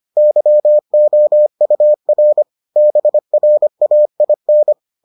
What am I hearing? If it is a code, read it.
Morse code, heard as a keyed tone, about 25 wpm, 600 Hz. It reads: YOUR BRAIN